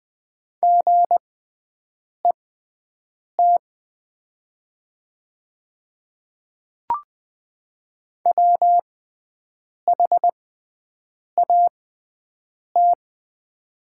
Morse code, heard as a keyed tone, about 20 wpm, spaced as for 7 wpm, 700 Hz. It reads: GET WHAT
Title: 6x Character Spacing - Top 200 Words - 20wpm